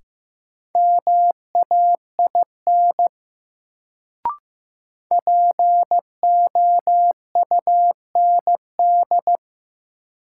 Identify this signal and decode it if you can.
Morse code, heard as a keyed tone, about 15 wpm, 700 Hz. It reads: MAIN POUND
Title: Sentences from Top 1000 Words - Review Part 2 - New Word Drill - 15wpm